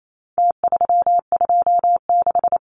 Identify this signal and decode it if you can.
Morse code, heard as a keyed tone, about 28 wpm, 700 Hz. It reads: T326
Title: Numbers - Contest Style - 4-Digit 28wpm